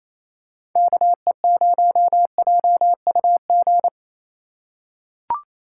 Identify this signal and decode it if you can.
Morse code, heard as a keyed tone, about 28 wpm, 700 Hz. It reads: KE0JUG
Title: USA Call Signs - 6 Characters - 2x Character Spacing - 28wpm